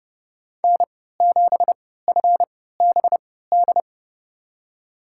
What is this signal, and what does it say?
Morse code, heard as a keyed tone, about 30 wpm, spaced as for 17 wpm, 700 Hz. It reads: N7FBD
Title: USA Call Signs - 5 Characters - 3x Character Spacing - 30wpm